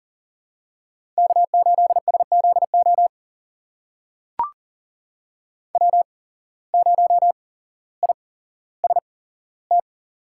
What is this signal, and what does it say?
Morse code, heard as a keyed tone, about 40 wpm, spaced as for 17 wpm, 700 Hz. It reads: K8SZO W0IST
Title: USA Call Signs - 5 Characters - 8x Character Spacing - 40wpm